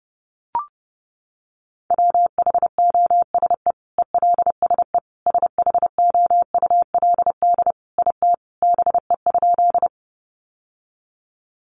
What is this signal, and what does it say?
Morse code, heard as a keyed tone, about 30 wpm, 700 Hz. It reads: WHOSE ELSE SHOULD IT BE?
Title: Sentences from Top 1200 Words 30wpm